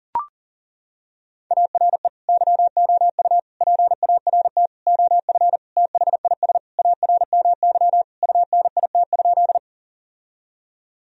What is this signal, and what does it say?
Morse code, heard as a keyed tone, about 40 wpm, 700 Hz. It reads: ARE YOU PART OF THIS ARMY UNIT?
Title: Sentences from Top 1000 Words - Review Part 2 - 40wpm